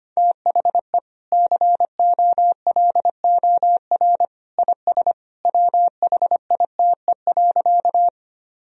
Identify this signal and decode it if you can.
Morse code, heard as a keyed tone, about 25 wpm, 700 Hz. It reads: THE COLOR IS WHITE.
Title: Sentences from Top 1000 Words - Review Part 2 - 25wpm